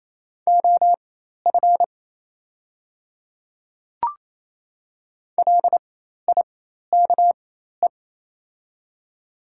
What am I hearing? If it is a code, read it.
Morse code, heard as a keyed tone, about 28 wpm, spaced as for 13 wpm, 700 Hz. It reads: OF LIKE
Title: 4x Character Spacing - Top 100 Words - 28wpm